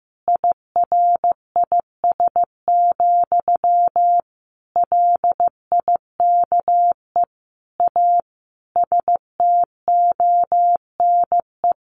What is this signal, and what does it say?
Morse code, heard as a keyed tone, about 15 wpm, 700 Hz. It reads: IRIS, LIKE A STONE
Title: A Sound Of Thunder 15wpm